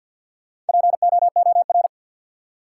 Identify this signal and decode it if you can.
Morse code, heard as a keyed tone, about 52 wpm, 700 Hz. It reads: POOR
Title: Four Letter Words - 50wpm